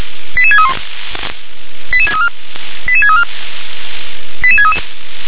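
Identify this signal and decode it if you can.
Five-tone selective call (ZVEI), heard as quick runs of five tones, tones of about 80 ms. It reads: 88631 88634 88634 88633
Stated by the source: ZVEI five-tone selective call 88742 received via SDR.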